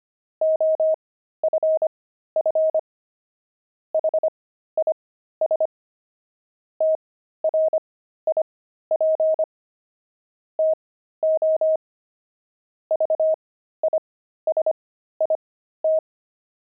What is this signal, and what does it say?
Morse code, heard as a keyed tone, about 25 wpm, spaced as for 13 wpm, 630 Hz. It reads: OFF HIS TRIP TO VISIT
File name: pangram-fs-13-cs-25-kidding